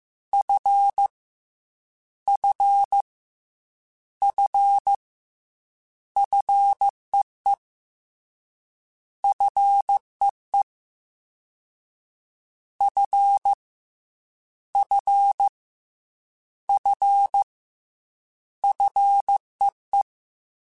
Morse code, heard as a keyed tone, about 15 wpm, 800 Hz. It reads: F F F FEE FEE F F F FEE
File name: Fs / Single F mp3s / Single F Prefixes 3x2 Twice @13